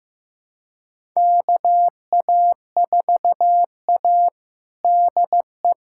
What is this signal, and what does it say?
Morse code, heard as a keyed tone, about 15 wpm, 700 Hz. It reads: KA4A DE
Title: USA Callsign DE Callsign - 4 Characters - 15wpm